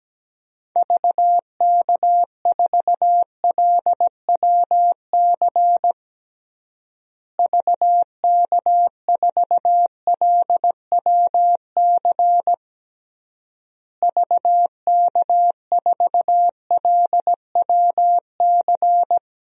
Morse code, heard as a keyed tone, about 17 wpm, 700 Hz. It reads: VK4LWC VK4LWC VK4LWC
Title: DXCC-100 Call Signs - 6 Characters - Repeated 3x - 17wpm